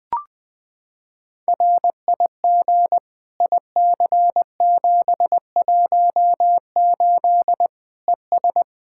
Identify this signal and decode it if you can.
Morse code, heard as a keyed tone, about 21 wpm, 700 Hz. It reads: RIG IC718 ES
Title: Practice QSO - Exchange 2 - Info - 20wpm